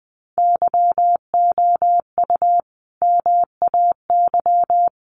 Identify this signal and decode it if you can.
Morse code, heard as a keyed tone, about 19 wpm, 700 Hz. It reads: YOU MAY
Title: Sentences from Top 700 Words 20wpm